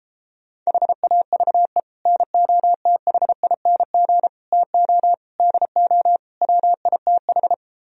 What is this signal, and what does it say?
Morse code, heard as a keyed tone, about 33 wpm, 700 Hz. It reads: HAVE NOTHING TO DO WITH